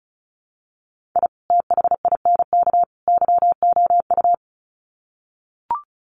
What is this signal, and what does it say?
Morse code, heard as a keyed tone, about 34 wpm, 700 Hz. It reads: I THINK YOU
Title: Sets of 3 Words 35wpm